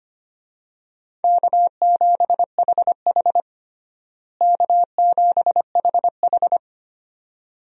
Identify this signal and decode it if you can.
Morse code, heard as a keyed tone, about 25 wpm, 700 Hz. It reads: K7HH K7HH
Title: USA Call Signs - 4 Characters - Repeated 2x - 25wpm